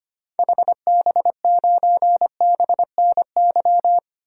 Morse code, heard as a keyed tone, about 25 wpm, 700 Hz. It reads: HB9BNY
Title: DXCC-100 Call Signs - 6 Characters - 25wpm